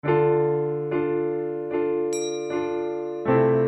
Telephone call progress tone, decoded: dial tone